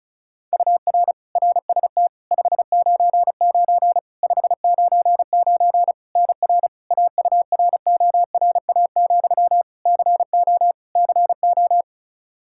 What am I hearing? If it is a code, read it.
Morse code, heard as a keyed tone, about 36 wpm, 700 Hz. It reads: UR RST 599 599 NR AURORA, CO CO
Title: K1USN SST QSO Practice 35wpm